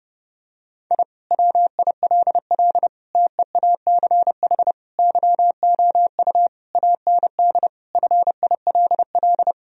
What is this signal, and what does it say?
Morse code, heard as a keyed tone, about 30 wpm, 700 Hz. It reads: I WILL TEACH YOU AND FILL
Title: Sentences from Top 1000 Words - Review Part 3 - 30wpm